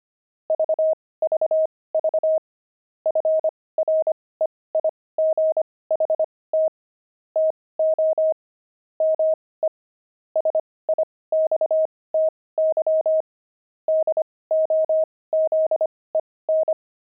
Morse code, heard as a keyed tone, about 25 wpm, spaced as for 18 wpm, 630 Hz. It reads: VVV FREIGHT TO ME SIXTY DOZEN
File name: pangram-fs-18-cs-25-twelve